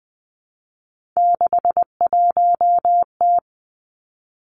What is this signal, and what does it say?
Morse code, heard as a keyed tone, about 20 wpm, 700 Hz. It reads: 61T
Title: Three Challenging Letter-Number Combinations 20wpm